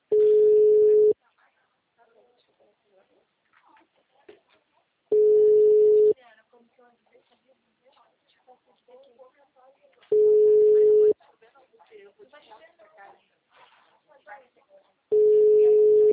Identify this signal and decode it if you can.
Ringback tone (Europe, 425 Hz), with speech.